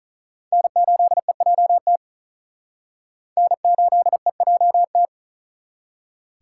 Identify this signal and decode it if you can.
Morse code, heard as a keyed tone, about 42 wpm, 700 Hz. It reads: N8EJT N8EJT
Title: Speed Racing - USA Call Signs 35wpm